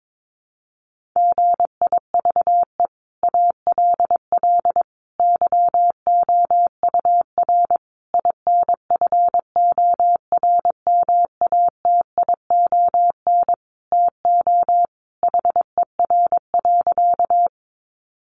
Morse code, heard as a keyed tone, about 22 wpm, 700 Hz. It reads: GIVE ALL YOUR INFORMATION TO HER.
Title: Sentences from Top 600 Words 22wpm